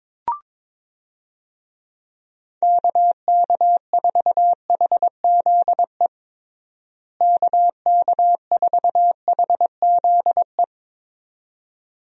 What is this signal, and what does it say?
Morse code, heard as a keyed tone, about 22 wpm, 700 Hz. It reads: KK4HZE KK4HZE
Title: USA Call Signs - 6 Characters - Repeated 2x - 22wpm